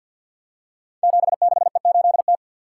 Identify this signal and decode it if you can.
Morse code, heard as a keyed tone, about 51 wpm, 700 Hz. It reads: 76E8T
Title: Code Groups - 5 Letter:Numbers - 50wpm